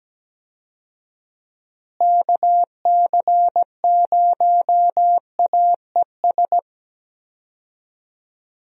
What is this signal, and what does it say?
Morse code, heard as a keyed tone, about 17 wpm, 700 Hz. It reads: KC0AES
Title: USA Call Signs - 6 Characters - Repeated 3x - 17wpm